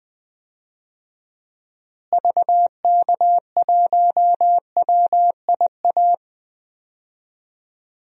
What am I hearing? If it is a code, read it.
Morse code, heard as a keyed tone, about 20 wpm, 700 Hz. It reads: VK1WIA
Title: DXCC-100 Call Signs - 6 Characters - Repeated 3x - 20wpm